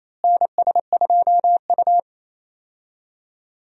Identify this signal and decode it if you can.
Morse code, heard as a keyed tone, about 28 wpm, 700 Hz. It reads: NS2U
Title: USA Call Signs - 4 Characters - Repeated 2x - 28wpm